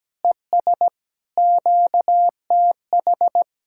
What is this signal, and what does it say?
Morse code, heard as a keyed tone, about 17 wpm, 700 Hz. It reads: ES QTH
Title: Speed-Racing Common QSO Phrases - 17wpm